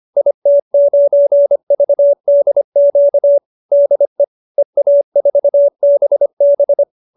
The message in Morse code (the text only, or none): IT9VDQ DE EA4BB